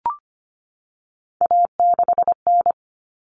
Morse code, heard as a keyed tone, about 25 wpm, 700 Hz. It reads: A6N
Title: All Cut Numbers - 3-Digit 25wpm